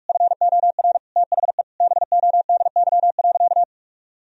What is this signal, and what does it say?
Morse code, heard as a keyed tone, about 41 wpm, 700 Hz. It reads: FOR THE BODY.
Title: Sentences from Top 1000 Words - Review Part 4 - 45wpm